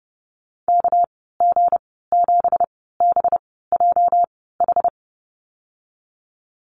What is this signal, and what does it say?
Morse code, heard as a keyed tone, about 30 wpm, spaced as for 17 wpm, 700 Hz. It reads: KG7BJH
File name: Lesson 37g - Letter x - Call Signs 30wpm